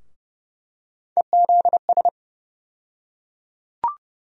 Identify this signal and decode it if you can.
Morse code, heard as a keyed tone, about 30 wpm, 700 Hz. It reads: EZS
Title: USA Call Sign Postfixes - 3 Letters 30wpm